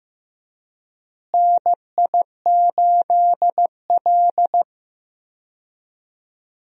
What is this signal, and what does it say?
Morse code, heard as a keyed tone, about 15 wpm, 700 Hz. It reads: NI8L
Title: USA Call Signs - 4 Characters - Repeated 3x - 15wpm